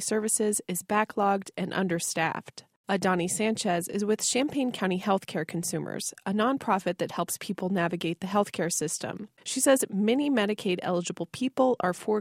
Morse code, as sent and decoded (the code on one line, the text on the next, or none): none